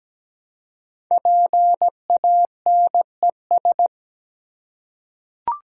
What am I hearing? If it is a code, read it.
Morse code, heard as a keyed tone, about 17 wpm, 700 Hz. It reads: PANES
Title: Code Groups - 5 Letters - 17wpm